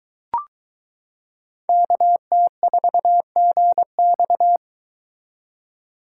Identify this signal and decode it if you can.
Morse code, heard as a keyed tone, about 23 wpm, 700 Hz. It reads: KT4GX